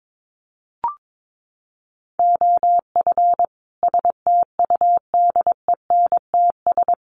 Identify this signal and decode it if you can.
Morse code, heard as a keyed tone, about 22 wpm, 700 Hz. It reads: OF STUDENTS
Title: Sets of 5 Words Encore 22wpm